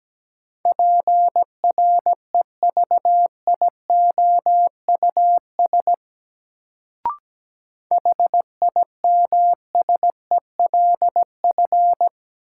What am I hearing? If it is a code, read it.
Morse code, heard as a keyed tone, about 17 wpm, 700 Hz. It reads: PREVIOUS HIMSELF
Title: Sentences from Top 1500 Words - New Word Drill - 17wpm